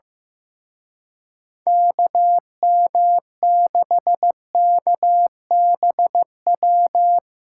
Morse code, heard as a keyed tone, about 16 wpm, 700 Hz. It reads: KM6KBW
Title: Speed Racing - USA Call Signs 15wpm